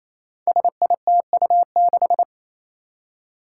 Morse code, heard as a keyed tone, about 27 wpm, 700 Hz. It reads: SITU6
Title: Five Challenging Letter-Number Combinations 28wpm